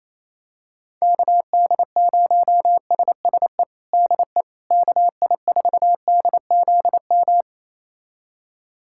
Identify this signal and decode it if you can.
Morse code, heard as a keyed tone, about 28 wpm, 700 Hz. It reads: KD0SSE DE KI4DZM
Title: USA Callsign DE Callsign 28wpm